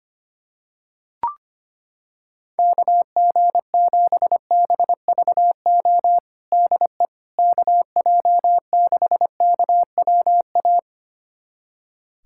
Morse code, heard as a keyed tone, about 25 wpm, 700 Hz. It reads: KG7BVO DE KJ6KWA
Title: USA Callsign DE Callsign - 6 Characters - 25wpm